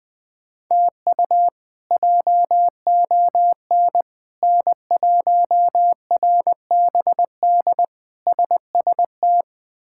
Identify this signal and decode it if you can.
Morse code, heard as a keyed tone, about 20 wpm, 700 Hz. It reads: TU JON N1RBD SST
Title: K1USN SST QSO Practice - 20wpm